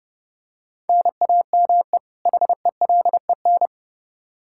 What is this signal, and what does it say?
Morse code, heard as a keyed tone, about 32 wpm, 700 Hz. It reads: NAME HELEN
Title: Practice QSO - Exchange 1 - Beginning - 30wpm